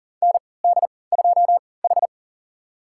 Morse code, heard as a keyed tone, about 37 wpm, spaced as for 22 wpm, 700 Hz. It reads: ND2H
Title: USA Call Signs - 4 Characters - 3x Character Spacing - 40wpm